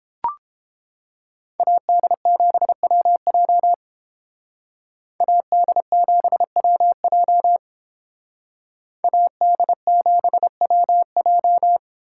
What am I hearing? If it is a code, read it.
Morse code, heard as a keyed tone, about 30 wpm, 700 Hz. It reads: AD7WJ AD7WJ AD7WJ